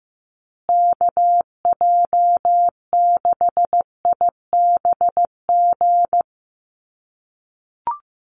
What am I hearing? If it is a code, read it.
Morse code, heard as a keyed tone, about 15 wpm, 700 Hz. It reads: KJ6IBG
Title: USA Call Signs Repeated 2x 15wpm